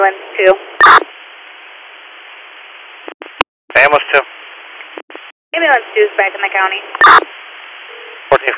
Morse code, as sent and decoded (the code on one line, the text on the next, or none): none